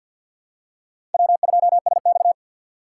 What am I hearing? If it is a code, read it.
Morse code, heard as a keyed tone, about 45 wpm, 700 Hz. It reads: W2SX
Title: USA Call Signs - 4 Characters - 2x Character Spacing - 50wpm